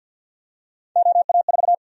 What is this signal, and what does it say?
Morse code, heard as a keyed tone, about 51 wpm, 700 Hz. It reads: OA4
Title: USA Call Sign Prefixes - 50wpm